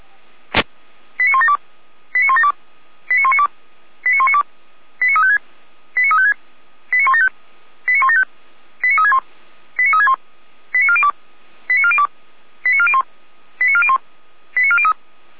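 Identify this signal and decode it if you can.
Five-tone selective call (ZVEI), heard as quick runs of five tones, tones of about 70 ms. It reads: 89172 89172 89192 89192 89356 89356 89175 89175 89361 89361 89402 89402 89401 89401 89403
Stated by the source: Five-tone ZVEI selective call received by SDR.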